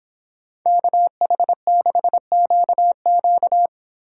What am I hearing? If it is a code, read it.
Morse code, heard as a keyed tone, about 26 wpm, 700 Hz. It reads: KH6QQ